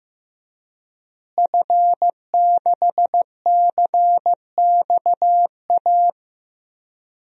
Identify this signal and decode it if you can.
Morse code, heard as a keyed tone, about 15 wpm, 700 Hz. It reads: F6CXA
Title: DXCC-100 Call Signs - 5 Characters - Repeated 3x - 15wpm